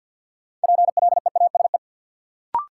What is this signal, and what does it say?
Morse code, heard as a keyed tone, about 52 wpm, 700 Hz. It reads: PLEASE
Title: Sentences from Top 1500 Words - New Word Drill - 50wpm